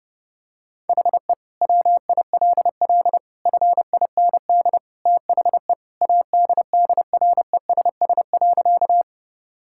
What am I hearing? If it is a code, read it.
Morse code, heard as a keyed tone, about 32 wpm, 700 Hz. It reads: HE WILL FIND THE ADDRESS.
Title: Sentences from Top 1000 Words - Review Part 6 - 30wpm